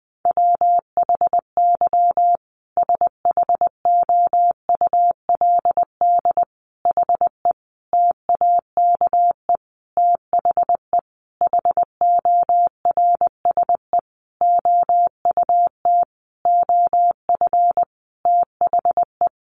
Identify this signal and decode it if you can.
Morse code, heard as a keyed tone, about 20 wpm, 700 Hz. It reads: WHY SHOULD HE TAKE THE HORSE OUT OF THE
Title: Sentences from Top 1500 Words 20wpm